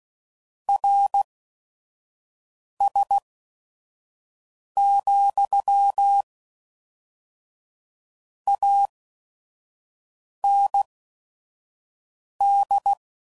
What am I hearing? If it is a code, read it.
Morse code, heard as a keyed tone, about 16 wpm, spaced as for 5 wpm, 800 Hz. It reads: RS, AND
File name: CW Code Groups @5WPM / Technical Subjects / AND Gate